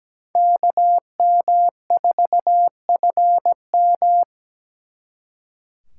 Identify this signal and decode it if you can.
Morse code, heard as a keyed tone, about 20 wpm, 700 Hz. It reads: KM4FM
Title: Speed Racing - USA Call Signs 17wpm